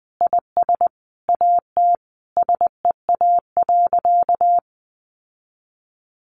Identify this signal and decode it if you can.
Morse code, heard as a keyed tone, about 20 wpm, 700 Hz. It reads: IS AT SEA.
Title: Sentences from Top 1400 Words 20wpm